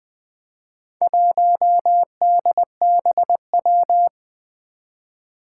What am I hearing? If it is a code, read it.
Morse code, heard as a keyed tone, about 20 wpm, 700 Hz. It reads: 1DBW
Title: Four Challenging Letter-Number Combinations 20wpm